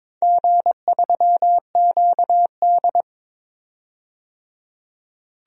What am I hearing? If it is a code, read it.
Morse code, heard as a keyed tone, about 22 wpm, 700 Hz. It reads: G3QD
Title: DXCC-100 Call Signs - 4 Characters - 22wpm